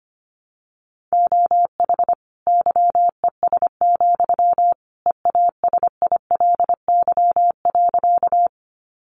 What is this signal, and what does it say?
Morse code, heard as a keyed tone, about 25 wpm, 700 Hz. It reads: OH YES, EASILY.
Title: Sentences from Top 900 Words 25wpm